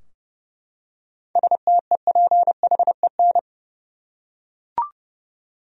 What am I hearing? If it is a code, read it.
Morse code, heard as a keyed tone, about 30 wpm, 700 Hz. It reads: STEPHEN
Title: USA Common Names - 30wpm